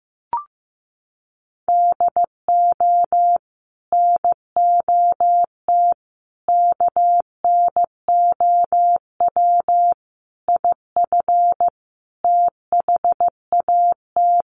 Morse code, heard as a keyed tone, about 15 wpm, 700 Hz. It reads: DO NOT KNOW IF THAT
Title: Sets of 5 Words Encore2 15wpm